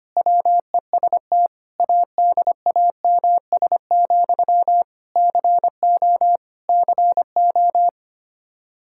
Morse code, heard as a keyed tone, about 25 wpm, 700 Hz. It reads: WEST ADAMS, CO CO
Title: Practice QSO - Exchange 1 - Beginning - 25wpm